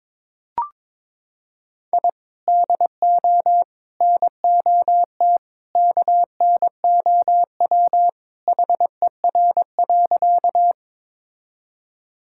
Morse code, heard as a keyed tone, about 22 wpm, 700 Hz. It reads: I DO NOT KNOW HER.
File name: practice / 22wpm / Sentences with 2-Word Phrases - 22wpm